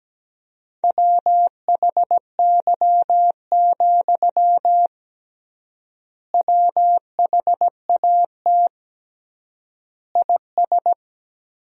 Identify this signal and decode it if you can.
Morse code, heard as a keyed tone, about 17 wpm, 700 Hz. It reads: WHY, WHAT IS
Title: 3x Word-Spacing - Sentences from Top 200 Words - 17wpm